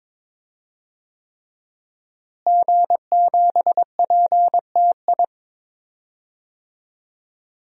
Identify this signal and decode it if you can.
Morse code, heard as a keyed tone, about 22 wpm, 700 Hz. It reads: G7PTI
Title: DXCC-100 Call Signs - 5 Characters - Repeated 3x - 22wpm